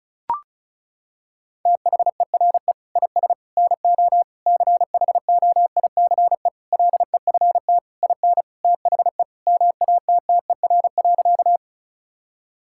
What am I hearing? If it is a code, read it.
Morse code, heard as a keyed tone, about 36 wpm, 700 Hz. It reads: THERE IS NO CHOICE LEFT IN THE MATTER.
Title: Sentences from Top 1000 Words - Review Part 6 - 35wpm